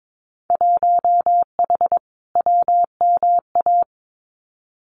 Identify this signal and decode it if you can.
Morse code, heard as a keyed tone, about 22 wpm, 700 Hz. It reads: 1H WMA
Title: Winter Field Day Exchanges 22wpm